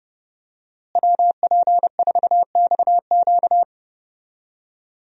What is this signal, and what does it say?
Morse code, heard as a keyed tone, about 31 wpm, 700 Hz. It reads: WP4XQ
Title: USA Call Signs - 5 Characters - Repeated 2x - 30wpm